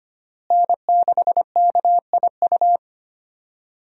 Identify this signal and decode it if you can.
Morse code, heard as a keyed tone, about 25 wpm, 700 Hz. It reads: N6KIU
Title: USA Call Signs - 5 Characters - Repeated 3x - 25wpm